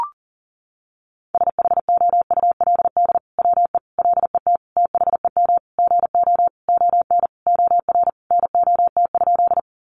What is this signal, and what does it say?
Morse code, heard as a keyed tone, about 40 wpm, 700 Hz. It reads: SHOULD WE LET THEM GO ON OR NOT?
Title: Sentences from Top 600 Words 40wpm